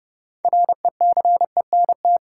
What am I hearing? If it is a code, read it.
Morse code, heard as a keyed tone, about 30 wpm, 700 Hz. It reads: RECENT
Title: Sentences from Top 1400 Words - New Word Drill Alt - 30wpm